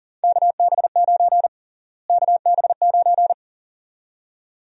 Morse code, heard as a keyed tone, about 39 wpm, 700 Hz. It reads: KB9 KB9
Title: USA Call Sign Prefixes - Repeated 2x - 40wpm